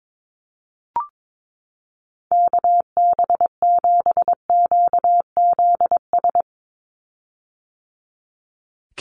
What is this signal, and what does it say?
Morse code, heard as a keyed tone, about 22 wpm, 700 Hz. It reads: KB7QZS
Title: USA Call Signs 22wpm